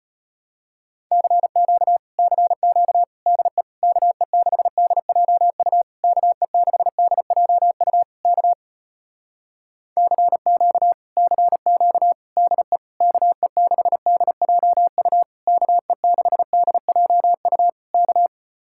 Morse code, heard as a keyed tone, about 33 wpm, 700 Hz. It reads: CQ CQ DE KE6DJU KE6DJU K CQ CQ DE KE6DJU KE6DJU K